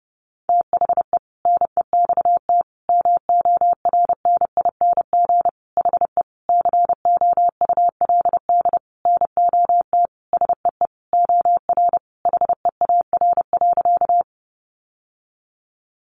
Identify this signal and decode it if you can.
Morse code, heard as a keyed tone, about 30 wpm, 700 Hz. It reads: THE NEXT MORNING HE COULD NOT SEE OR HEAR.
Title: Sentences from Top 600 Words 30wpm